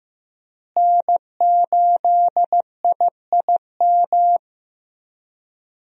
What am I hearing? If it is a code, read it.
Morse code, heard as a keyed tone, about 15 wpm, 700 Hz. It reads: N8IIM
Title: USA Call Signs - 5 Characters - Repeated 2x - 15wpm